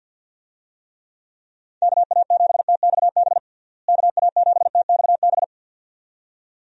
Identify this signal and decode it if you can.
Morse code, heard as a keyed tone, about 50 wpm, 700 Hz. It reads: KA7TXB KA7TXB
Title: USA Call Signs - 6 Characters - Repeated 2x - 50wpm